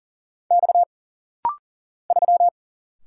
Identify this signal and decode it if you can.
Morse code, heard as a keyed tone, about 39 wpm, 700 Hz. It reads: X 3
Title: Single Letter-Number Alt - Rapid-Fire 40wpm